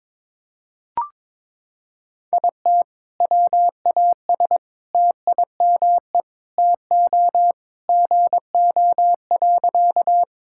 Morse code, heard as a keyed tone, about 22 wpm, 700 Hz. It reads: IT WAS TIME TO GO.
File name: practice / 22wpm / Sentences with 4-Word Phrases - 22wpm